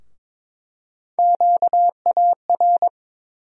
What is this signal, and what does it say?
Morse code, heard as a keyed tone, about 22 wpm, 700 Hz. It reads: QAR
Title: USA Call Sign Postfixes - 3 Letters 22wpm